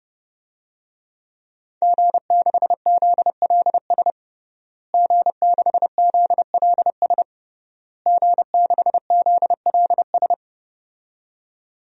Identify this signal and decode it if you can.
Morse code, heard as a keyed tone, about 32 wpm, 700 Hz. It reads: G6ZLS G6ZLS G6ZLS
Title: DXCC-100 Call Signs - 5 Characters - Repeated 3x - 30wpm